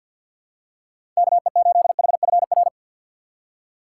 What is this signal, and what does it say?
Morse code, heard as a keyed tone, about 50 wpm, 700 Hz. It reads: KE8HFR
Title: USA Call Signs - 6 Characters - 6x Character Spacing - 50wpm